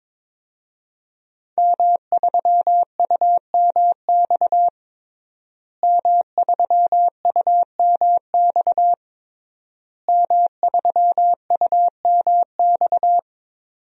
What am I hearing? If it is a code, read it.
Morse code, heard as a keyed tone, about 22 wpm, 700 Hz. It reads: M3UMX M3UMX M3UMX
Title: DXCC-100 Call Signs - 5 Characters - Repeated 3x - 22wpm